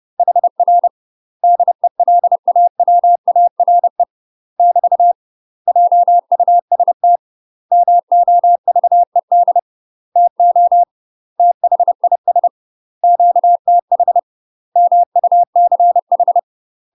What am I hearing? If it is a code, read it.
Morse code, heard as a keyed tone, about 30 wpm, 700 Hz. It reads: HR DELAWARE <BT> JUST MOVED TO THIS QTH MUCH
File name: UK MASTER QSOs TRAINING / 30WPM / UK MASTER QSOs 0035